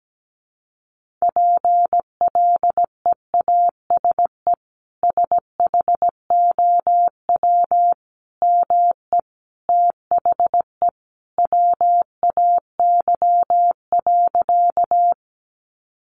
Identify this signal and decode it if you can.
Morse code, heard as a keyed tone, about 17 wpm, 700 Hz. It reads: PLEASE SHOW ME THE WAY.
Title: Sentences from Top 1500 Words 17wpm